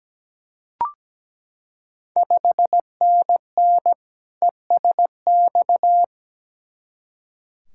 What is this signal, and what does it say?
Morse code, heard as a keyed tone, about 17 wpm, 700 Hz. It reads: 5NN ESX
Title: Ontario QSO Party Exchange 17wpm